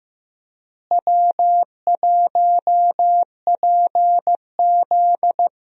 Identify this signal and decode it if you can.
Morse code, heard as a keyed tone, about 15 wpm, 700 Hz. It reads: W1PZ
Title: USA Call Signs - 4 Characters - Repeated 2x - 15wpm